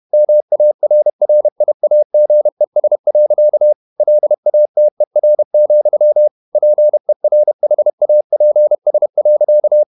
Morse code, heard as a keyed tone, about 31 wpm, 600 Hz. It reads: MARRIAGES. LATER, PERHAPS.
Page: Never 31wpm | CW med Gnister